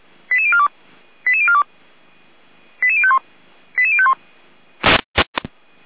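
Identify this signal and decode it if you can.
Five-tone selective call (ZVEI), heard as quick runs of five tones, tones of about 70 ms. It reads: 88042 88042 88051 88051
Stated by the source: ZVEI five-tone call 88103 received on SDR.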